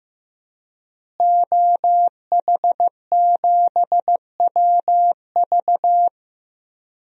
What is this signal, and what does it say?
Morse code, heard as a keyed tone, about 15 wpm, 700 Hz. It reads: OH7WV
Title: DXCC-100 Call Signs - 5 Characters - Repeated 3x - 15wpm